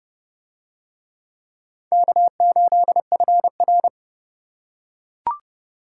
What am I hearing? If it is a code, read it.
Morse code, heard as a keyed tone, about 30 wpm, 700 Hz. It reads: K8FR